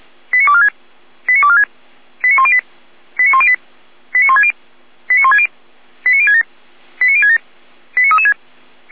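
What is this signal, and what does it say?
Five-tone selective call (ZVEI), heard as quick runs of five tones, tones of about 70 ms. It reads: 89246 89246 89108 89108 89160 89160 89067 89067 89336
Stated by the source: ZVEI five-tone call 89170 received by SDR.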